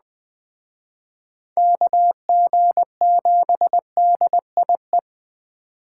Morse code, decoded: KG7DIE

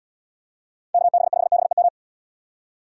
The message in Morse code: KB5BEM

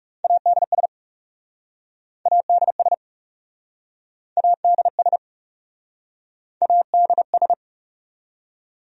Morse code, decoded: ADS ADS ADS ADS

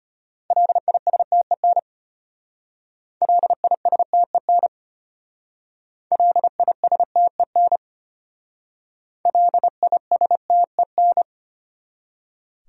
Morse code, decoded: LISTEN LISTEN LISTEN LISTEN

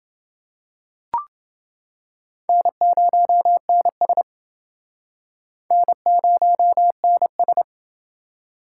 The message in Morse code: N0NS N0NS

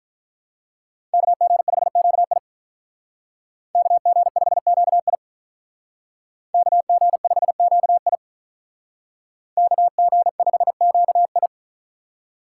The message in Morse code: KG5QI KG5QI KG5QI KG5QI

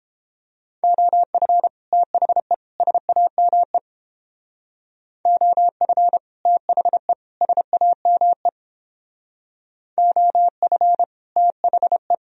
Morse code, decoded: OF THE SAME OF THE SAME OF THE

